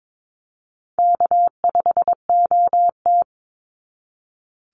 Morse code, heard as a keyed tone, about 22 wpm, 700 Hz. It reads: K5OT
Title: NA SOTA Chaser Callsigns 22wpm